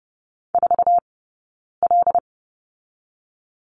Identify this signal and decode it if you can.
Morse code, heard as a keyed tone, about 30 wpm, 700 Hz. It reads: 4 L
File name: Lesson 18d - Letter m - Two Characters 30wpm